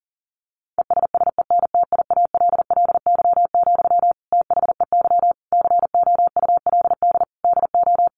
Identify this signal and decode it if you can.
Morse code, heard as a keyed tone, about 40 wpm, 700 Hz. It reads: ESSENTIALLY, THEY COULD DO
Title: Sentences from Top 1300 Words 40wpm